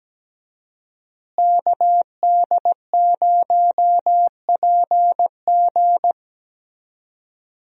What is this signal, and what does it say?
Morse code, heard as a keyed tone, about 17 wpm, 700 Hz. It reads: KD0PG
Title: USA Call Signs - 5 Characters - Repeated 3x - 17wpm